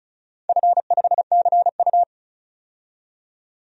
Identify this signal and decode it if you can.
Morse code, heard as a keyed tone, about 36 wpm, 700 Hz. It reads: F5CU
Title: DXCC-100 Call Signs - 4 Characters - 35wpm